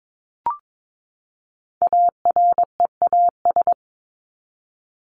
Morse code, heard as a keyed tone, about 22 wpm, 700 Hz. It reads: AREAS
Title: ES:S Suffix - Top 100 Words with Base Words 22wpm